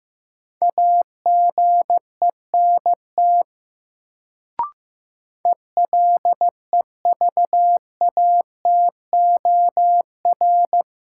Morse code, heard as a keyed tone, about 15 wpm, 700 Hz. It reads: AGENT ELEVATOR
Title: Sentences from Top 1600 Words - New Word Drill Alt - 15wpm